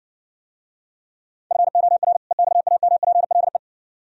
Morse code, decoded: FOR EXAMPLE

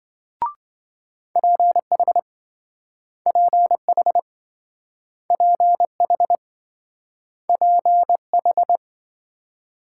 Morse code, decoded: PH PH PH PH

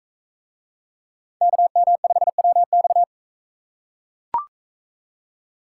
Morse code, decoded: KM5WX

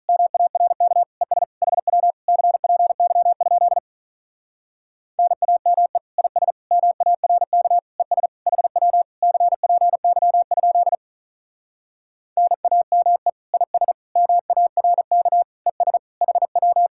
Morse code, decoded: MARK ES HW CPY? NAME IS MARK ES HW CPY? NAME IS MARK ES HW

